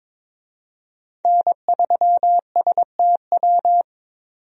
Morse code, heard as a keyed tone, about 22 wpm, 700 Hz. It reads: N3STW
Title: USA Call Signs - 5 Characters - 22wpm